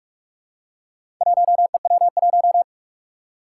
Morse code, heard as a keyed tone, about 46 wpm, 700 Hz. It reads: 1EW1
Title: Four Challenging Letter-Number Combinations 45wpm